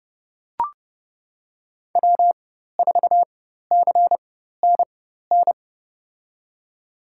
Morse code, heard as a keyed tone, about 31 wpm, spaced as for 14 wpm, 700 Hz. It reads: W4CNN
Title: USA Call Signs - 4x Character Spacing - 30wpm